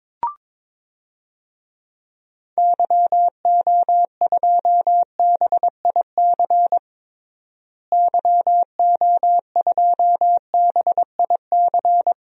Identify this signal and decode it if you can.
Morse code, heard as a keyed tone, about 21 wpm, 700 Hz. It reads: YO2BIC YO2BIC
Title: DXCC-100 Call Signs - 6 Characters - Repeated 3x - 22wpm